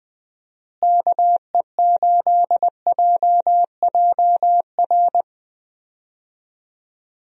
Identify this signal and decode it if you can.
Morse code, heard as a keyed tone, about 21 wpm, 700 Hz. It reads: KE8JJR
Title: USA Call Signs - 6 Characters - Repeated 3x - 20wpm